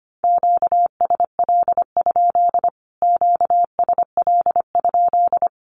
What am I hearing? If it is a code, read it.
Morse code, heard as a keyed tone, about 25 wpm, 700 Hz. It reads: QSL? QSL?
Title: Q-Codes Repeated 3x 25wpm